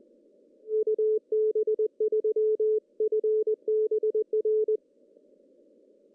Morse code, heard as a keyed tone, about 20 wpm, 430 Hz. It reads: UB3FBR
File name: CW / cqww-cw-k5tr